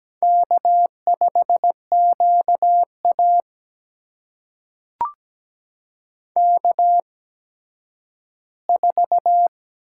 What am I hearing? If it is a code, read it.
Morse code, heard as a keyed tone, about 17 wpm, spaced as for 10 wpm, 700 Hz. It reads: K5QA K4
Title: USA Call Signs - 4 Characters - 8x Character Spacing - 17wpm